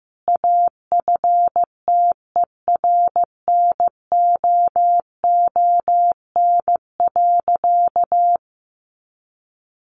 Morse code, heard as a keyed tone, about 15 wpm, 700 Hz. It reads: AFTERNOON.
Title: Sentences from Top 1500 Words 15wpm